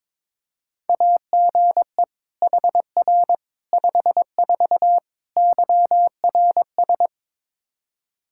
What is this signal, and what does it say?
Morse code, heard as a keyed tone, about 23 wpm, 700 Hz. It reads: AGE HR 54 YRS
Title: Practice QSO - Exchange 3 - More Info - 22wpm